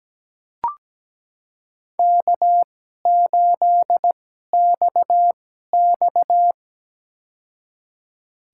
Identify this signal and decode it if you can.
Morse code, heard as a keyed tone, about 17 wpm, spaced as for 12 wpm, 700 Hz. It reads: K8XX
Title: USA Call Signs - 4 Characters - 2x Character Spacing - 17wpm